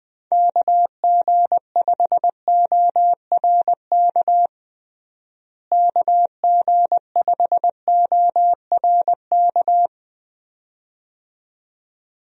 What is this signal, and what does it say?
Morse code, heard as a keyed tone, about 20 wpm, 700 Hz. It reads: KG5ORK KG5ORK
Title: USA Call Signs - 6 Characters - Repeated 2x - 20wpm